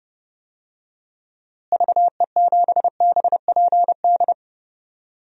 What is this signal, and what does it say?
Morse code, heard as a keyed tone, about 30 wpm, 700 Hz. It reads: VE7BPD
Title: DXCC-100 Call Signs - 6 Characters - Repeated 3x - 30wpm